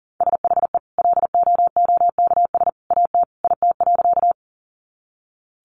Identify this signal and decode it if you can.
Morse code, heard as a keyed tone, about 40 wpm, 700 Hz. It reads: SHE LOOKS AT IT.
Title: Additional Easy Phrases 40wpm